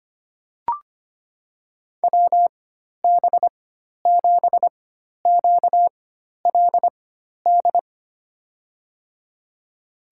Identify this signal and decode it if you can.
Morse code, heard as a keyed tone, about 25 wpm, spaced as for 12 wpm, 700 Hz. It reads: WB7QLD
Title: USA Call Signs - 6 Characters - 4x Character Spacing - 25wpm